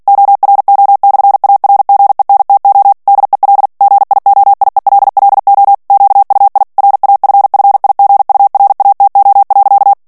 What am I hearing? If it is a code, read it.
Morse code, heard as a keyed tone, about 48 wpm, 800 Hz. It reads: ORO/ARGENTO DEL GIOIELLO QUI RAFFIGURATO.
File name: morse mp3 / ITALIEN texte clair 1 vitesse 45 mots minute